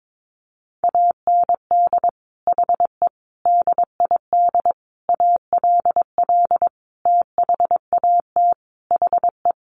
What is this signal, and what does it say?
Morse code, heard as a keyed tone, about 22 wpm, 700 Hz. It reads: AND HE DID ALL THAT HE
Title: Sentences from Top 1500 Words 22wpm